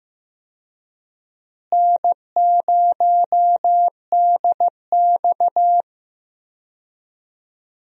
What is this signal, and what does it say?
Morse code, heard as a keyed tone, about 15 wpm, 700 Hz. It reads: N0DX
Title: USA Call Signs - 4 Characters - Repeated 2x - 15wpm